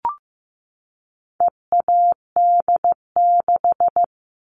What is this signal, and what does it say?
Morse code, heard as a keyed tone, about 15 wpm, 700 Hz. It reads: EAD6
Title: All Cut Numbers - 4-Digit 15wpm